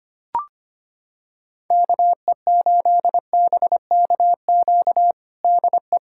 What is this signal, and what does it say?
Morse code, heard as a keyed tone, about 25 wpm, 700 Hz. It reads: KE8BKQ DE
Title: USA Callsign DE Callsign - 6 Characters - 25wpm